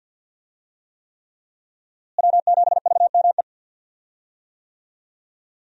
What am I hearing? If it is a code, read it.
Morse code, heard as a keyed tone, about 51 wpm, 700 Hz. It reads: W7VME
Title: USA Call Signs - 5 Characters - Repeated 3x - 50wpm